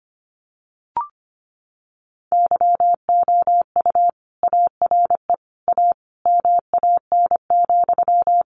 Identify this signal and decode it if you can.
Morse code, heard as a keyed tone, about 25 wpm, 700 Hz. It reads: YOU ARE A MAN,
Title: Sentences from Top 200 Words 25wpm